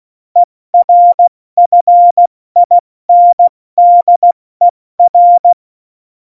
Morse code, heard as a keyed tone, about 16 wpm, 700 Hz. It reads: ERFINDER
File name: Morsekurs / Witze / Witz 26. Januar 2021, 70 F80